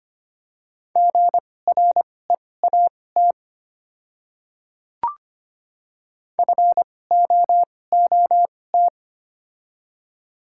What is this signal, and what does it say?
Morse code, heard as a keyed tone, about 25 wpm, spaced as for 18 wpm, 700 Hz. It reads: GREAT FOOT
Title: 2x Character Spacing - Top 500 Words - 25wpm